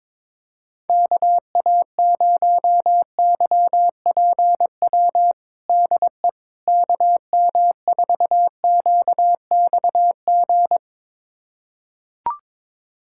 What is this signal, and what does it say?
Morse code, heard as a keyed tone, about 22 wpm, 700 Hz. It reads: KA0YPW DE KM4QXG
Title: USA Callsign DE Callsign - 6 Characters - 22wpm